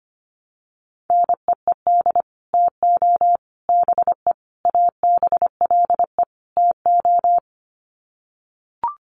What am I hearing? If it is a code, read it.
Morse code, heard as a keyed tone, about 25 wpm, 700 Hz. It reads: NEED TO BE ABLE TO
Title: Sets of 5 Words Encore 25wpm